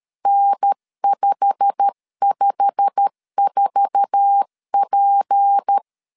Morse code, heard as a keyed tone, about 13 wpm, 800 Hz. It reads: N554P
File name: LEZIONI / 8 YJ45 / 8 YJ45 12wpm 2151